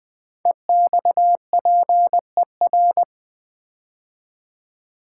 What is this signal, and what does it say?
Morse code, heard as a keyed tone, about 20 wpm, 700 Hz. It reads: EXPER
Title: Build A Word - 5 Letters At A Time - 12-Letter Words - 20wpm